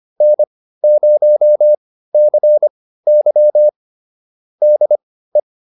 Morse code, heard as a keyed tone, about 25 wpm, spaced as for 15 wpm, 600 Hz. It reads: N0CY DE